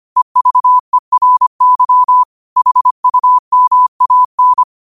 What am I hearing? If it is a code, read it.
Morse code, heard as a keyed tone, about 25 wpm, 1000 Hz. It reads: EVERY HUMAN
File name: qotdfiles / 25wpm